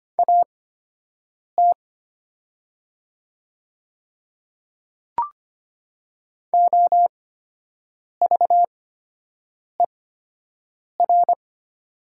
Morse code, heard as a keyed tone, about 25 wpm, spaced as for 7 wpm, 700 Hz. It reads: AT OVER
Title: 8x Character Spacing - Top 100 Words - 25wpm